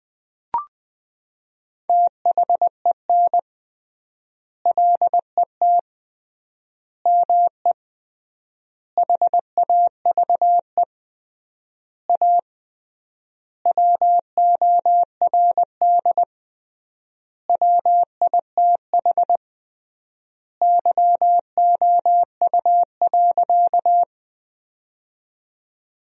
Morse code, heard as a keyed tone, about 20 wpm, 700 Hz. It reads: THEN LET ME HAVE A WORD WITH YOU.
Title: 3x Word-Spacing - Sentences from Top 300 Words - 20wpm